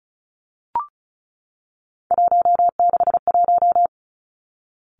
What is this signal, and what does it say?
Morse code, heard as a keyed tone, about 36 wpm, 700 Hz. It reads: 161
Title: Three Challenging Letter-Number Combinations 35wpm